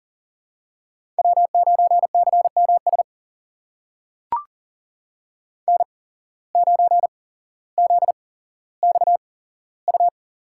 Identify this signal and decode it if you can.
Morse code, heard as a keyed tone, about 40 wpm, spaced as for 17 wpm, 700 Hz. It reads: W9CMS N9ZXU
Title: USA Call Signs - 5 Characters - 8x Character Spacing - 40wpm